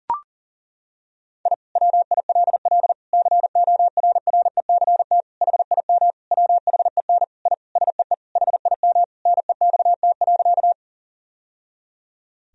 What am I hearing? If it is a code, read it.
Morse code, heard as a keyed tone, about 37 wpm, 700 Hz. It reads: I WILL CORRECT HIM WHEN I SEE HIM NEXT.
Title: Sentences from Top 1000 Words - Review Part 2 - 40wpm